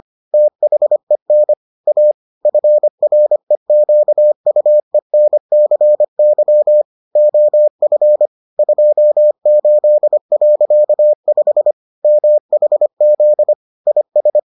Morse code, heard as a keyed tone, about 25 wpm, 600 Hz. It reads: THEN A FREQUENCY OF 28.5 MHZ IS